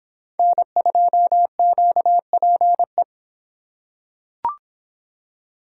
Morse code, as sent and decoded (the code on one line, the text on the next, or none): -. ..--- --.- .--. .
N2QPE